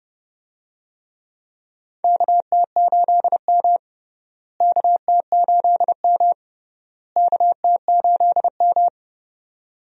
Morse code, heard as a keyed tone, about 29 wpm, 700 Hz. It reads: KT8M KT8M KT8M
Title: USA Call Signs - 4 Characters - Repeated 3x - 30wpm